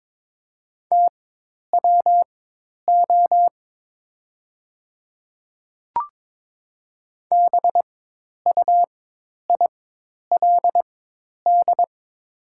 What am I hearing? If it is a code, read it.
Morse code, heard as a keyed tone, about 22 wpm, spaced as for 10 wpm, 700 Hz. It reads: TWO BUILD
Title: 4x Character Spacing - Top 200 Words - 22wpm